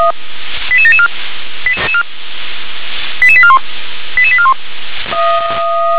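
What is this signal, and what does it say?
Five-tone selective call (ZVEI), heard as quick runs of five tones, tones of about 70 ms. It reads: 88774 88841 88841
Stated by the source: ZVEI five-tone call received on SDR.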